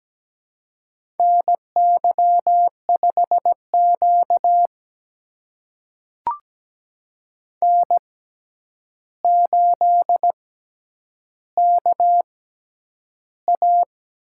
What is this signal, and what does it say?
Morse code, heard as a keyed tone, about 17 wpm, spaced as for 9 wpm, 700 Hz. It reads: NY5Q N8KA
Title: USA Call Signs - 4 Characters - 6x Character Spacing - 17wpm